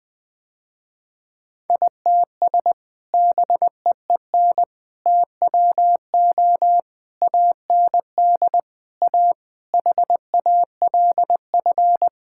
Morse code, heard as a keyed tone, about 20 wpm, 700 Hz. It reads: ITS BEEN TWO AND A HALF
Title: Sentences with 5-Word Phrases - Spoken Beginning - 20wpm